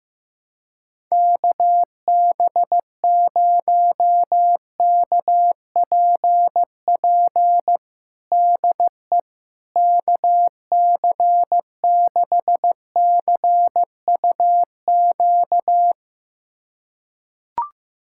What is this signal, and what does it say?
Morse code, heard as a keyed tone, about 16 wpm, 700 Hz. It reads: KB0KPP DE KC6CUQ
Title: USA Callsign DE Callsign 15wpm